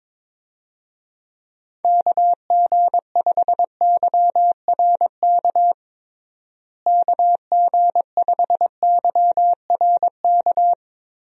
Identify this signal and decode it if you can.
Morse code, heard as a keyed tone, about 22 wpm, 700 Hz. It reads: KG5YRK KG5YRK
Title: USA Call Signs - 6 Characters - Repeated 3x - 22wpm